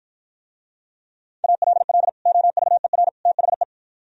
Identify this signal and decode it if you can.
Morse code, heard as a keyed tone, about 52 wpm, 700 Hz. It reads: ALL OVER THE